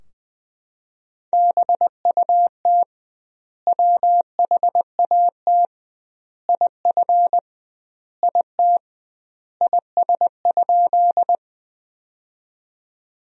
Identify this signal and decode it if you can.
Morse code, heard as a keyed tone, about 21 wpm, 700 Hz. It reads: BUT WHAT IF IT IS?
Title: 2x Word-Spacing - Sentences from Top 100 Words - 20wpm